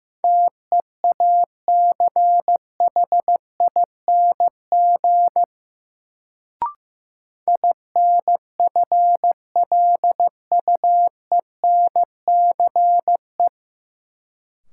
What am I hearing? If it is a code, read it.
Morse code, heard as a keyed tone, about 15 wpm, 700 Hz. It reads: TEACHING INFLUENCE
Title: Sentences from Top 1100 Words - New Word Drill - 15wpm